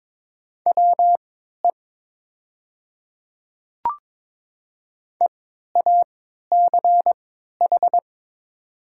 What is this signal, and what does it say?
Morse code, heard as a keyed tone, about 22 wpm, spaced as for 12 wpm, 700 Hz. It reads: WE EACH
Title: 3x Character Spacing - Top 100 Words - 22wpm